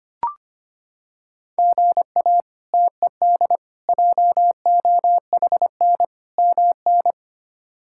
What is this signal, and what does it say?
Morse code, heard as a keyed tone, about 25 wpm, 700 Hz. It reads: GA TED JOHN MN
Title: K1USN SST QSO Practice - 25wpm